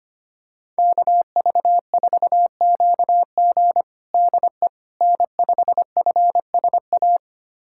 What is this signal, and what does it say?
Morse code, heard as a keyed tone, about 25 wpm, 700 Hz. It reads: KV4QG DE N5FSA
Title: USA Callsign DE Callsign - 5 Characters - 25wpm